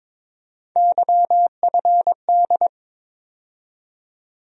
Morse code, heard as a keyed tone, about 22 wpm, 700 Hz. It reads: YFD
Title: USA Call Sign Postfixes - 3 Letters 22wpm